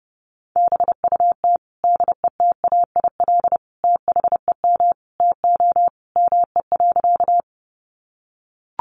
Morse code, heard as a keyed tone, about 30 wpm, 700 Hz. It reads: BUT DETAIL THEM TO ME.
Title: Sentences from Top 1500 Words 30wpm